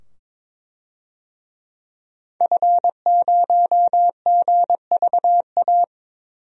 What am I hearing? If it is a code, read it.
Morse code, heard as a keyed tone, about 22 wpm, 700 Hz. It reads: F0GVA
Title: DXCC-100 Call Signs - 5 Characters - 22wpm